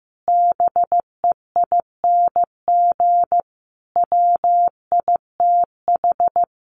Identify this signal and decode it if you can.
Morse code, heard as a keyed tone, about 15 wpm, 700 Hz. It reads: BEING WITH
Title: Sets of 5 Words 15wpm